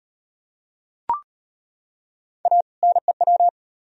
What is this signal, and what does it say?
Morse code, heard as a keyed tone, about 33 wpm, 700 Hz. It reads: A NEW